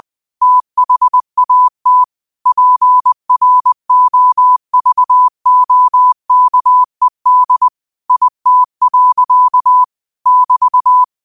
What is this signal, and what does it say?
Morse code, heard as a keyed tone, about 20 wpm, 1000 Hz. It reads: THAT PROVOKED IT. <BT>
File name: qotdfiles / 20wpm